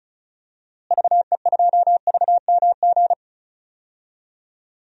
Morse code, heard as a keyed tone, about 34 wpm, 700 Hz. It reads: VE2VMG
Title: DXCC-100 Call Signs - 6 Characters - Repeated 3x - 35wpm